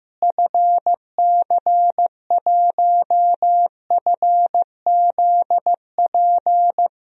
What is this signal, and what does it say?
Morse code, heard as a keyed tone, about 15 wpm, 700 Hz. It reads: FC1FZP
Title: DXCC-100 Call Signs - 6 Characters - Repeated 3x - 15wpm